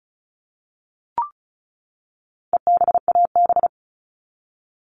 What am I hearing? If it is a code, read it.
Morse code, heard as a keyed tone, about 34 wpm, 700 Hz. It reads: EBAB
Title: All Cut Numbers - 4-Digit 35wpm